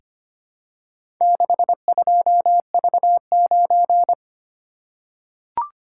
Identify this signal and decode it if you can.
Morse code, heard as a keyed tone, about 25 wpm, 700 Hz. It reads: 62V9
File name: practice / 25wpm / Four Letter-Number Combinations - 25wpm